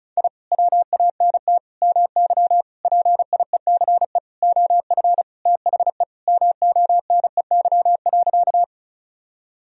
Morse code, heard as a keyed tone, about 34 wpm, 700 Hz. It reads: I WANT MY PIECE OF THE MONEY.
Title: Sentences from Top 1000 Words - Review Part 2 - 35wpm